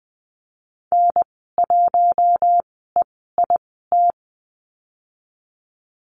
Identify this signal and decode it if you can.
Morse code, heard as a keyed tone, about 20 wpm, spaced as for 14 wpm, 700 Hz. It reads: N1EIT
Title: 069 Lesson 12g - Letter d - Call Signs 20wpm